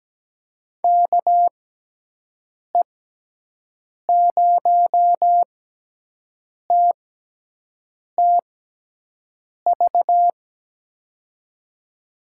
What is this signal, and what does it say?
Morse code, heard as a keyed tone, about 17 wpm, spaced as for 6 wpm, 700 Hz. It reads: KE0TTV
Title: USA Call Signs - 6 Characters - 6x Character Spacing - 17wpm